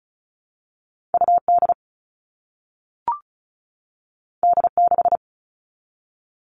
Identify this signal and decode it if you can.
Morse code, heard as a keyed tone, about 36 wpm, 700 Hz. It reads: UD D6
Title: All Cut Numbers - 2-Digit 35wpm